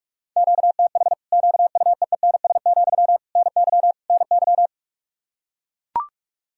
Morse code, heard as a keyed tone, about 41 wpm, 700 Hz. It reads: QTH QUEENS, NY NY
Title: Practice QSO - Exchange 1 - Beginning - 45wpm